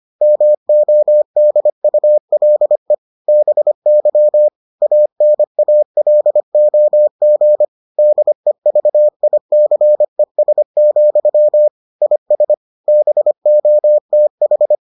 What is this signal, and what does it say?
Morse code, heard as a keyed tone, about 25 wpm, 600 Hz. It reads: MODULE BY ANALOG DEVICES, IS BOTH